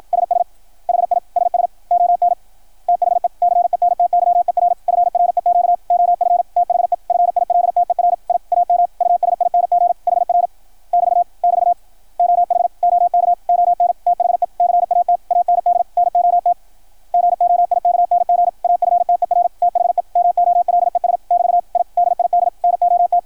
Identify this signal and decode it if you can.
Morse code, heard as a keyed tone, about 53 wpm, 700 Hz, with radio noise. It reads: SS HI SS ON THE CENTYER FREQ OF THE FILTER I AM USING HR <BT> <BT> OH OK ON THE CAT AND NOT GOIKNG AFTER THE MOLES <BT> I DID NOT
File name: archived / CRSnet